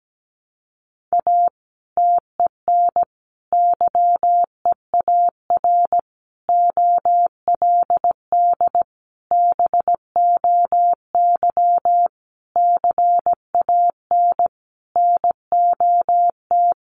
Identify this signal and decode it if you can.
Morse code, heard as a keyed tone, about 17 wpm, 700 Hz. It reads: A TEN YEAR OLD BOY CAN NOT
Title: Sentences from Top 1300 Words 17wpm